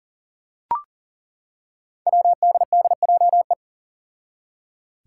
Morse code, heard as a keyed tone, about 40 wpm, 700 Hz. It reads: WDDJE
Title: Five Challenging Letter-Number Combinations 40wpm